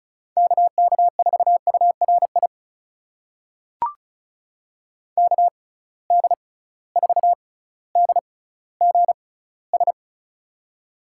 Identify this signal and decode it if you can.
Morse code, heard as a keyed tone, about 36 wpm, spaced as for 18 wpm, 700 Hz. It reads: KK4URI KD4DGS
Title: USA Call Signs - 6 Characters - 6x Character Spacing - 35wpm